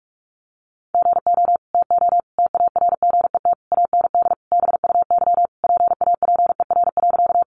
Morse code, heard as a keyed tone, about 44 wpm, 700 Hz. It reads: GO TO TARGET AND BUY PAPER.
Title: Sentences from Top 1200 Words 45wpm